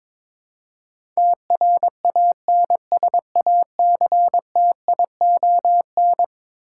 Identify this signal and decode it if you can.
Morse code, heard as a keyed tone, about 22 wpm, 700 Hz. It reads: TRANSACTION
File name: practice / 22wpm / Eleven Letter Words - 22wpm